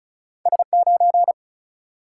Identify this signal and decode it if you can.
Morse code, heard as a keyed tone, about 36 wpm, 700 Hz. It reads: S9
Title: Single-Letter Single-Number 35wpm